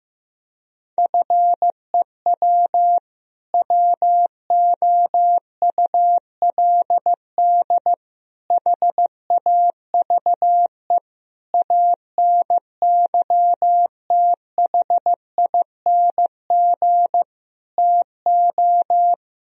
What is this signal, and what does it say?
Morse code, heard as a keyed tone, about 15 wpm, 700 Hz. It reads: FEW WOULD HAVE ANYTHING TO
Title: Sentences from Top 1000 Words - Review Part 1 - 15wpm